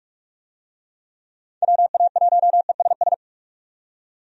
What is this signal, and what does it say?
Morse code, heard as a keyed tone, about 46 wpm, 700 Hz. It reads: WA1ESS
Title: USA Call Signs - 6 Characters - 45wpm